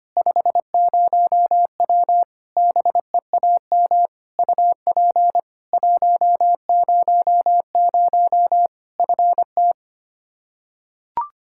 Morse code, heard as a keyed tone, about 25 wpm, 700 Hz. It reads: 50W BEAM UP 100 FT
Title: Practice QSO - Exchange 2 - Info - 25wpm